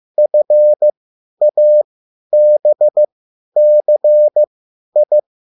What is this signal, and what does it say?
Morse code, heard as a keyed tone, about 15 wpm, spaced as for 10 wpm, 600 Hz. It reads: FABCI